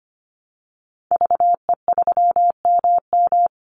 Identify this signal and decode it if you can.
Morse code, heard as a keyed tone, about 25 wpm, 700 Hz. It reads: VE3MM
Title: Call Signs - 25wpm